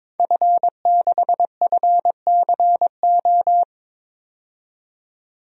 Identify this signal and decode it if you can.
Morse code, heard as a keyed tone, about 22 wpm, 700 Hz. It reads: F6FCO
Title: DXCC-100 Call Signs - 5 Characters - 22wpm